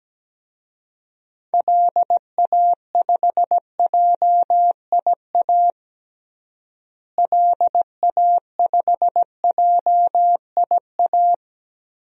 Morse code, heard as a keyed tone, about 17 wpm, 700 Hz. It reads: LA5JIA LA5JIA
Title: DXCC-100 Call Signs - 6 Characters - Repeated 2x - 17wpm